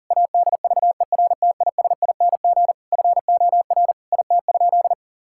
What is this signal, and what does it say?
Morse code, heard as a keyed tone, about 37 wpm, 700 Hz. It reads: ADVERTISING FOR IT?
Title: Sentences from Top 1000 Words - Review Part 6 - 40wpm